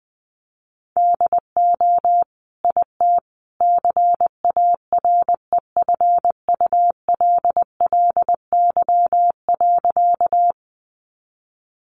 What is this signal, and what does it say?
Morse code, heard as a keyed tone, about 20 wpm, 700 Hz. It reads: DO IT CAREFULLY.
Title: Sentences from Top 1300 Words 20wpm